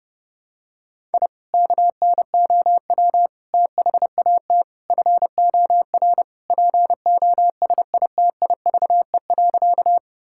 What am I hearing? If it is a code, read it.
Morse code, heard as a keyed tone, about 30 wpm, 700 Hz. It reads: I KNOW THAT FOR POSITIVE.
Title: Sentences from Top 1000 Words - Review Part 5 - 30wpm